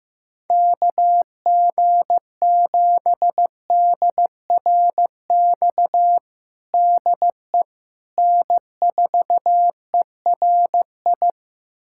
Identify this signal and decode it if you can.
Morse code, heard as a keyed tone, about 16 wpm, 700 Hz. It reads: KG7DRX DE N4ERI
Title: USA Callsign DE Callsign 15wpm